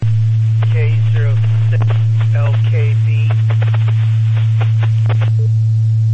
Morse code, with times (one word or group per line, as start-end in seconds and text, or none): none